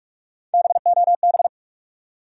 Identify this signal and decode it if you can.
Morse code, heard as a keyed tone, about 46 wpm, 700 Hz. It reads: BOB
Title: USA Common Names - Max 3 Letters - 45wpm